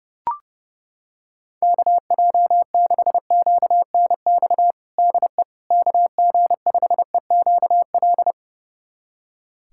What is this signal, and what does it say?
Morse code, heard as a keyed tone, about 32 wpm, 700 Hz. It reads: KJ6QNX DE KG5EQL
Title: USA Callsign DE Callsign - 6 Characters - 30wpm